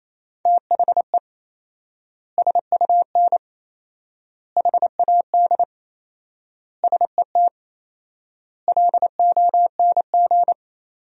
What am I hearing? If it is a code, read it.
Morse code, heard as a keyed tone, about 27 wpm, 700 Hz. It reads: THE SUN HAD SET LONG
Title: 4x Word-Spacing - Sentences from Top 300 Words - 28wpm